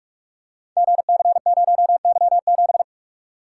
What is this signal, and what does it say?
Morse code, heard as a keyed tone, about 42 wpm, 700 Hz. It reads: GC0Y7
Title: Code Groups - 5 Letter:Numbers - 45wpm